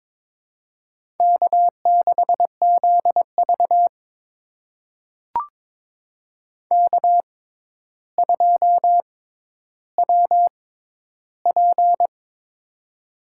Morse code, heard as a keyed tone, about 22 wpm, spaced as for 11 wpm, 700 Hz. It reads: K6ZV K2WP
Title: USA Call Signs - 4 Characters - 6x Character Spacing - 22wpm